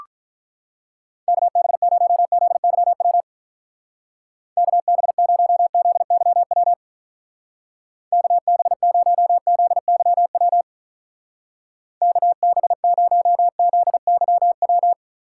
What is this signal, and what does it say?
Morse code, heard as a keyed tone, about 43 wpm, 700 Hz. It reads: KB0ZYW KB0ZYW KB0ZYW KB0ZYW
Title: Speed Racing - USA 2x3 Call Signs 35wpm